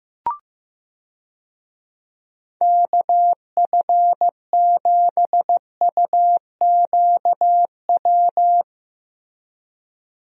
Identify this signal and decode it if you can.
Morse code, heard as a keyed tone, about 15 wpm, 700 Hz. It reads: KF7UQW
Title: USA Call Signs - 6 Characters - Repeated 2x - 15wpm